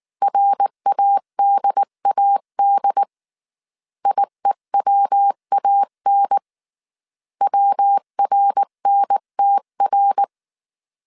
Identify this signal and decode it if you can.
Morse code, heard as a keyed tone, about 19 wpm, 800 Hz. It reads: RADAD IEWAN WRNTR
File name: CW / DET / 1 3 1500C S B 18wpm 1916